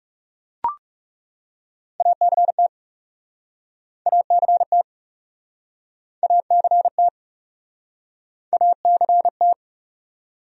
Morse code, heard as a keyed tone, about 40 wpm, 700 Hz. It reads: ACT ACT ACT ACT